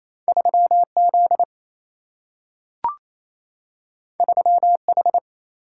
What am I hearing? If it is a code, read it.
Morse code, heard as a keyed tone, about 28 wpm, 700 Hz. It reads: 3Z 3H
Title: Single-Letter Single-Number 28wpm